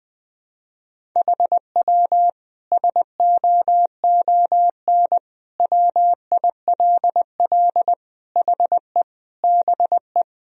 Morse code, heard as a keyed tone, about 21 wpm, 700 Hz. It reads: HW SOON WILL HE BE
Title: Sentences from Top 400 Words and Abbreviations 20wpm